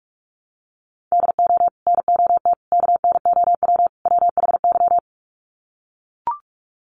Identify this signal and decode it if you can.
Morse code, heard as a keyed tone, about 44 wpm, 700 Hz. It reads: DO NOT KNOW WHY
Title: Sets of 4 Words 45wpm